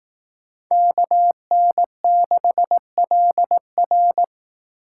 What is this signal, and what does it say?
Morse code, heard as a keyed tone, about 18 wpm, 700 Hz. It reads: KN6LR
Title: Speed Racing - USA 2x2 Call Signs 15wpm